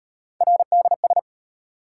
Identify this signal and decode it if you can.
Morse code, heard as a keyed tone, about 34 wpm, 700 Hz. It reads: RDS